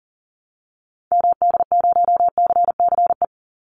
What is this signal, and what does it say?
Morse code, heard as a keyed tone, about 38 wpm, 700 Hz. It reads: MD0CCE
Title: Call Signs Repeated 2x - 40wpm